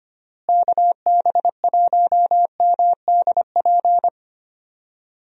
Morse code, heard as a keyed tone, about 25 wpm, 700 Hz. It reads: KB1MDP
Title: USA Call Signs - 6 Characters - Repeated 2x - 25wpm